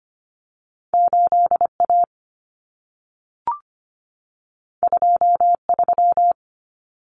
Morse code, heard as a keyed tone, about 25 wpm, 700 Hz. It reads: 8A 23
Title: Common Cut Numbers - 2-Digit 25wpm